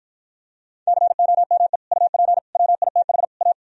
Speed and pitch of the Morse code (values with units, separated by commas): 52 wpm, 700 Hz